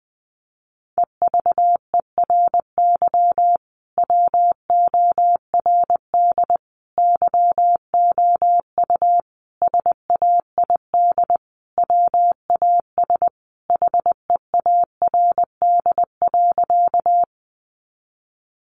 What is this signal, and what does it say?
Morse code, heard as a keyed tone, about 20 wpm, 700 Hz. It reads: EVERY WORD YOU SAID WAS HEARD.
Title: Sentences from Top 400 Words 20wpm